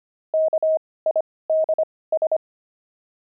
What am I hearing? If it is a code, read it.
Morse code, heard as a keyed tone, about 25 wpm, spaced as for 18 wpm, 630 Hz. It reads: KIDS
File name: pangram-fs-18-cs-25-peeves